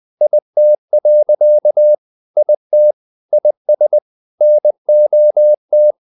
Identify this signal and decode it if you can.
Morse code, heard as a keyed tone, about 20 wpm, 600 Hz. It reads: IT. IT IS NOT
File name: the old man and the sea part 2 20wpm